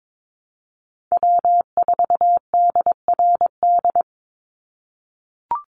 Morse code, heard as a keyed tone, about 22 wpm, 700 Hz. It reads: W4DRD
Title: USA Call Signs 22wpm